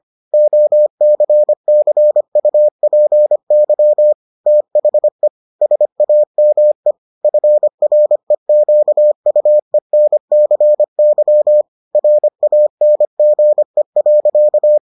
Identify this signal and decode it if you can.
Morse code, heard as a keyed tone, about 25 wpm, 600 Hz. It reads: OCCUPY THE SAME FREQUENCY RANGE.